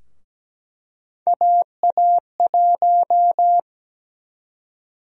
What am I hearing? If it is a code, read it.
Morse code, heard as a keyed tone, about 17 wpm, 700 Hz. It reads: AA1
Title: USA Call Sign Prefixes - Repeated 2x - 17wpm